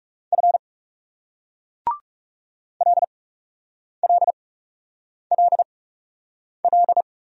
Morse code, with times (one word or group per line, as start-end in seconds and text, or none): 0.3-0.6 F
2.8-7.0 LLLL